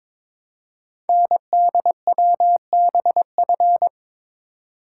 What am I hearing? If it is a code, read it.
Morse code, heard as a keyed tone, about 23 wpm, 700 Hz. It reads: NDWBF
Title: Code Groups - 5 Letters - 22wpm